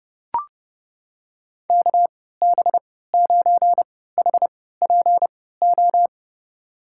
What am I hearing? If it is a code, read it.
Morse code, heard as a keyed tone, about 32 wpm, spaced as for 17 wpm, 700 Hz. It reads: KB9HPO
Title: USA Call Signs - 6 Characters - 3x Character Spacing - 30wpm